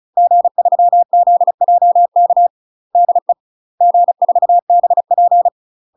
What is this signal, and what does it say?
Morse code, heard as a keyed tone, about 34 wpm, 700 Hz. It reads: G3ZJK DE G4BP
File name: UK MASTER QSOs TRAINING / 35WPM / UK MASTER QSOs 0021